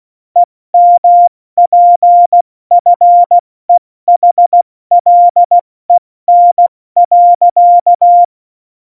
Morse code, heard as a keyed tone, about 16 wpm, 700 Hz. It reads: EMPFEHLEN.
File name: Morsekurs / Witze / Witz 26 Mai, 2020 70 F80